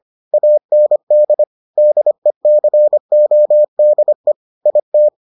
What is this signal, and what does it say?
Morse code, heard as a keyed tone, about 25 wpm, 600 Hz. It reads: AND DECODE IT